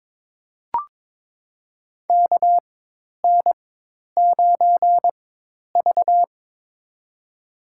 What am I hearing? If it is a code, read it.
Morse code, heard as a keyed tone, about 22 wpm, spaced as for 10 wpm, 700 Hz. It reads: KN9V
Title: USA Call Signs - 4 Characters - 4x Character Spacing - 22wpm